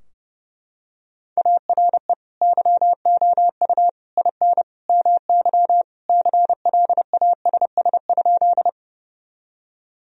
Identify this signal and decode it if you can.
Morse code, heard as a keyed tone, about 30 wpm, 700 Hz. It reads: ARE YOU IN MY CLASS?
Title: Sentences from Top 1000 Words - Review Part 2 - 30wpm